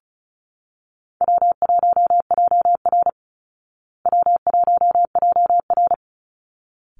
Morse code, heard as a keyed tone, about 34 wpm, 700 Hz. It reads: W1JR W1JR
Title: Easy Call Signs Repeated 2x - 35wpm